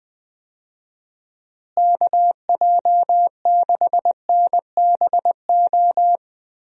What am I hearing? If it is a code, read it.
Morse code, heard as a keyed tone, about 20 wpm, 700 Hz. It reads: KJ6NBO
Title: USA Call Signs - 6 Characters - Repeated 2x - 20wpm